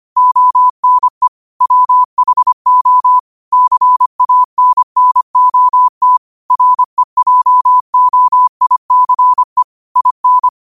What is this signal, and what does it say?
Morse code, heard as a keyed tone, about 25 wpm, 1000 Hz. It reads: ONE WHO CANNOT REJOICE IN